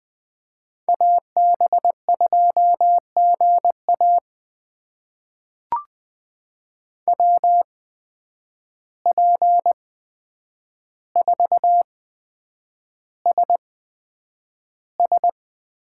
Morse code, heard as a keyed tone, about 20 wpm, spaced as for 9 wpm, 700 Hz. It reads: AB2GA WP4SS
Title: USA Call Signs - 5 Characters - 8x Character Spacing - 20wpm